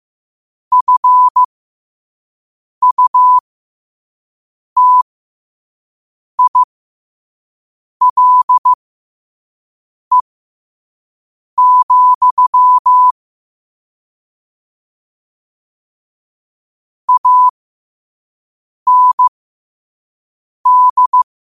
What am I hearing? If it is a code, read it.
Morse code, heard as a keyed tone, about 15 wpm, spaced as for 5 wpm, 1000 Hz. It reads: FUTILE, AND
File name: qotdfiles / 5wpm